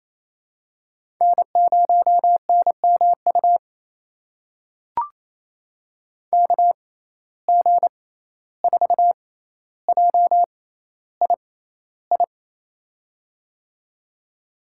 Morse code, heard as a keyed tone, about 28 wpm, spaced as for 14 wpm, 700 Hz. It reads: N0NMU KG4JII
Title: USA Call Signs - 6x Character Spacing - 28wpm